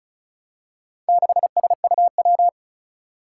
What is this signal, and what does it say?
Morse code, heard as a keyed tone, about 36 wpm, 700 Hz. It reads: 6SUW
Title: Four Challenging Letter-Number Combinations 35wpm